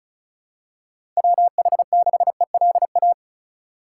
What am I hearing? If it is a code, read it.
Morse code, heard as a keyed tone, about 36 wpm, 700 Hz. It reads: WH6ELA
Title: USA Call Signs - 6 Characters - 2x Character Spacing - 35wpm